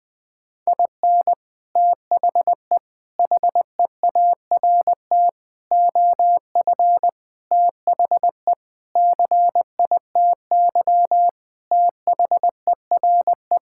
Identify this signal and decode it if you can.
Morse code, heard as a keyed tone, about 20 wpm, 700 Hz. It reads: IN THE HEART OF THE CITY THERE
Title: Sentences with 5-Word Phrases - 20wpm